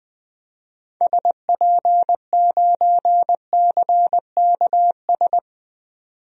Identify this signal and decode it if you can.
Morse code, heard as a keyed tone, about 20 wpm, 700 Hz. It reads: SP9CKS
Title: DXCC-100 Call Signs - 6 Characters - 20wpm